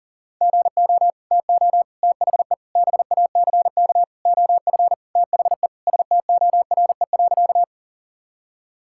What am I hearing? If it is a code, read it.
Morse code, heard as a keyed tone, about 40 wpm, 700 Hz. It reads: GO TO THE BACK OF THE STORE.
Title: Sentences from Top 1000 Words - Review Part 5 - 40wpm